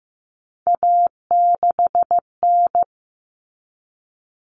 Morse code, heard as a keyed tone, about 15 wpm, 700 Hz. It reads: A6N
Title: All Cut Numbers - 3-Digit 15wpm